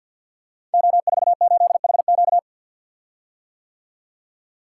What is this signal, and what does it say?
Morse code, heard as a keyed tone, about 52 wpm, 700 Hz. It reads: O48HQ
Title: Code Groups - 5 Letter:Numbers - 50wpm